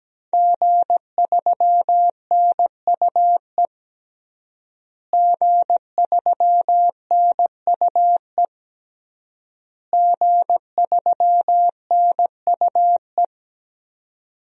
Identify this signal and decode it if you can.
Morse code, heard as a keyed tone, about 17 wpm, 700 Hz. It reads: G3NUE G3NUE G3NUE
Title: DXCC-100 Call Signs - 5 Characters - Repeated 3x - 17wpm